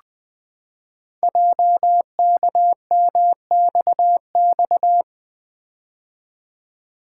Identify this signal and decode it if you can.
Morse code, heard as a keyed tone, about 20 wpm, 700 Hz. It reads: JKMXX
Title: Code Groups - 5 Letters - 20wpm